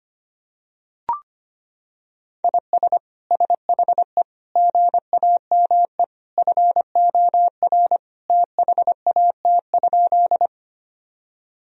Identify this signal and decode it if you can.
Morse code, heard as a keyed tone, about 25 wpm, 700 Hz. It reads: IS SHE GAME FOR THAT?
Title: Sentences from Top 1000 Words - Review Part 3 - 25wpm